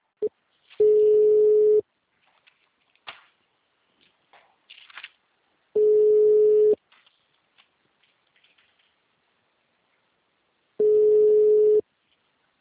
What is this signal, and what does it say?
Ringback tone (Europe, 425 Hz).
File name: out- -2005- - - .544